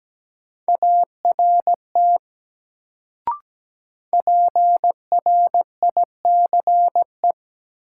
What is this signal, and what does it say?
Morse code, heard as a keyed tone, about 17 wpm, 700 Hz. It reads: ART PRICE
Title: Sentences from Top 600 Words - New Word Drill - 17wpm